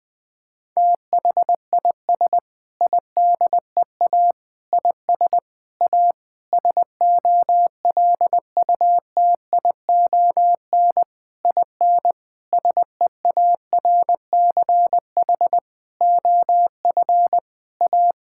Morse code, heard as a keyed tone, about 20 wpm, 700 Hz. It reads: THIS IDEA IS A SOLUTION IN SEARCH OF A
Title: Sentences with 8-Word Phrases - 20wpm